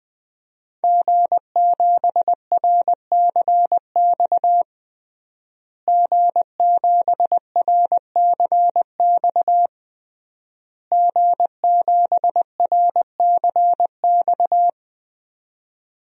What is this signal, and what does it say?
Morse code, heard as a keyed tone, about 20 wpm, 700 Hz. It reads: G7RCX G7RCX G7RCX
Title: DXCC-100 Call Signs - 5 Characters - Repeated 3x - 20wpm